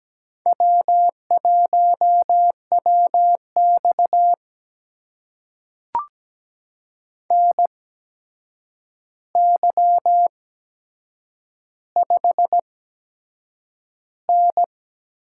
Morse code, heard as a keyed tone, about 17 wpm, spaced as for 7 wpm, 700 Hz. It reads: W1WX NY5N
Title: USA Call Signs - 4 Characters - 8x Character Spacing - 17wpm